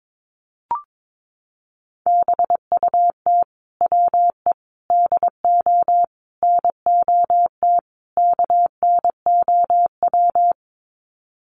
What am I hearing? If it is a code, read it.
Morse code, heard as a keyed tone, about 22 wpm, 700 Hz. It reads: BUT WE DO NOT KNOW
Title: Sets of 5 Words Encore2 22wpm